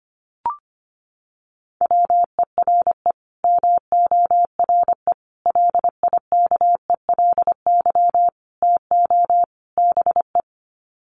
Morse code, heard as a keyed tone, about 25 wpm, 700 Hz. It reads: WERE MORE LIKELY TO BE
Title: Sets of 5 Words Encore2 25wpm